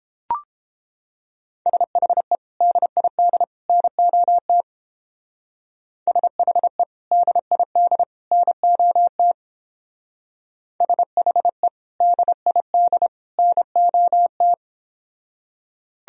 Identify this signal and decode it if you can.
Morse code, heard as a keyed tone, about 29 wpm, 700 Hz. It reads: SHE DID NOT SHE DID NOT SHE DID NOT